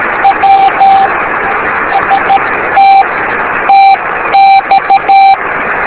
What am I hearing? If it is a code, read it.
Morse code, heard as a keyed tone, about 13 wpm, spaced as for 11 wpm, 780 Hz, with noise. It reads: W ST TX
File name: beacons / w5rp-b